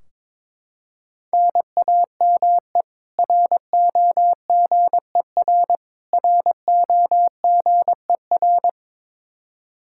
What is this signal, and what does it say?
Morse code, heard as a keyed tone, about 22 wpm, 700 Hz. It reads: NAME ROGER ROGER
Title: Practice QSO - Exchange 1 - Beginning - 22wpm